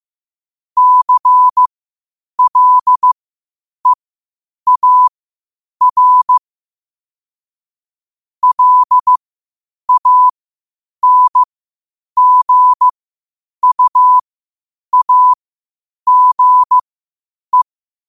Morse code, heard as a keyed tone, about 15 wpm, spaced as for 8 wpm, 1000 Hz. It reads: CLEAR LANGUAGE
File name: qotdfiles / 8wpm